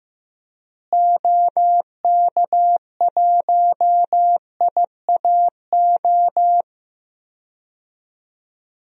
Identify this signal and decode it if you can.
Morse code, heard as a keyed tone, about 15 wpm, 700 Hz. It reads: OK1IAO
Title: DXCC-100 Call Signs - 6 Characters - Repeated 3x - 15wpm